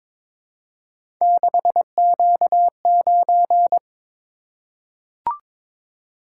Morse code, heard as a keyed tone, about 22 wpm, 700 Hz. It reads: 6Q9
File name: practice / 22wpm / Three Letter-Number Combinations - 22wpm